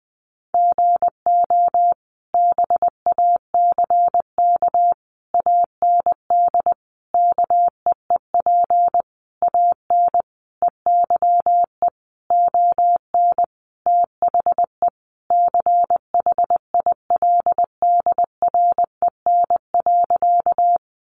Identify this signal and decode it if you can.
Morse code, heard as a keyed tone, about 19 wpm, 700 Hz. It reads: GO BACK AND KEEP AN EYE ON THE CHILDREN.
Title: Sentences from Top 300 Words 20wpm